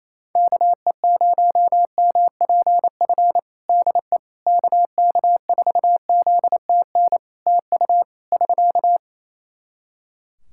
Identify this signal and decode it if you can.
Morse code, heard as a keyed tone, about 28 wpm, 700 Hz. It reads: KE0MPF DE KK4ZTN TU <SK>
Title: Practice QSO - Exchange 4 - Ending - 28wpm